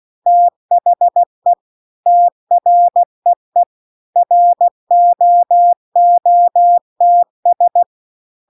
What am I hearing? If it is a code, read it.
Morse code, heard as a keyed tone, about 16 wpm, 700 Hz. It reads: THE TREE ROOTS